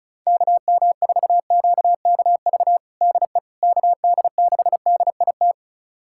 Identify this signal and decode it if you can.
Morse code, heard as a keyed tone, about 34 wpm, 690 Hz. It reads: KM4QKV DE KD6DIT
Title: USA Callsign DE Callsign - 6 Characters - 35wpm